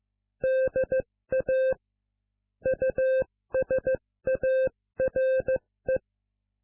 Morse code, heard as a keyed tone, about 15 wpm, 520 Hz. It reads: DA USARE